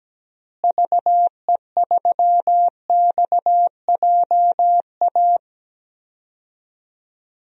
Morse code, heard as a keyed tone, about 17 wpm, 700 Hz. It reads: VE3XJA
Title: DXCC-100 Call Signs - 6 Characters - Repeated 2x - 17wpm